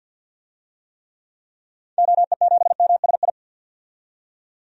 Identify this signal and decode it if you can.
Morse code, heard as a keyed tone, about 47 wpm, 700 Hz. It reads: OE7MSI
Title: DXCC-100 Call Signs - 6 Characters - 50wpm